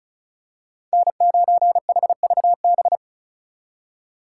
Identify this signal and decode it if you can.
Morse code, heard as a keyed tone, about 36 wpm, 700 Hz. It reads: N9HVB
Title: USA Call Signs - 5 Characters - 4x Character Spacing - 35wpm